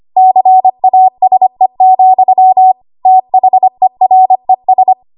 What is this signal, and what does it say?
Morse code, heard as a keyed tone, about 25 wpm, 750 Hz. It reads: CASE, THERES
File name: Archive / 25 WPM / 25WPM